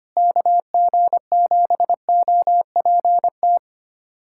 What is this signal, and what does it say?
Morse code, heard as a keyed tone, about 25 wpm, 700 Hz. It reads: KG7OPT
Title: USA Callsign DE Callsign - 6 Characters - 25wpm